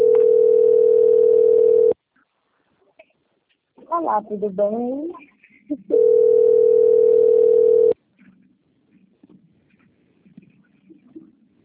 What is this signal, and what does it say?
Ringback tone (North America), 440 + 480 Hz, with speech.
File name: out- - - - - .23433